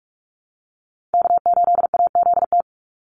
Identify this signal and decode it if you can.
Morse code, heard as a keyed tone, about 44 wpm, 700 Hz. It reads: K8AZT
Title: Call Signs Repeated 3x - 45wpm